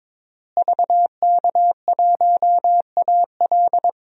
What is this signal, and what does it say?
Morse code, heard as a keyed tone, about 22 wpm, 700 Hz. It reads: VK1AL
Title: DXCC-100 Call Signs - 5 Characters - Repeated 3x - 22wpm